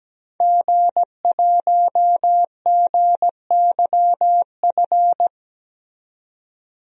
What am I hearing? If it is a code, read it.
Morse code, heard as a keyed tone, about 17 wpm, 700 Hz. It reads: G1GYF
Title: DXCC-100 Call Signs - 5 Characters - 17wpm